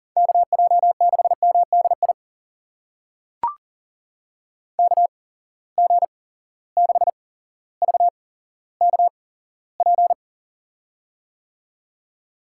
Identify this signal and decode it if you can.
Morse code, heard as a keyed tone, about 40 wpm, spaced as for 17 wpm, 700 Hz. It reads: KJ6MDI KG6VKP
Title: USA Call Signs - 6 Characters - 8x Character Spacing - 40wpm